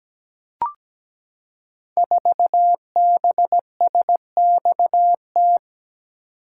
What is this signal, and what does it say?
Morse code, heard as a keyed tone, about 17 wpm, 700 Hz. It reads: 4BSXT
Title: Code Groups - 5 Letter:Numbers - 17wpm